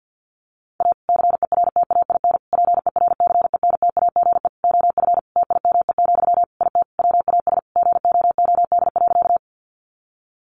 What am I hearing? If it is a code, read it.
Morse code, heard as a keyed tone, about 48 wpm, 700 Hz. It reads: A CERTAIN PERCENTAGE OF TIME, IT WAS GOOD.
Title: Sentences from Top 1300 Words 50wpm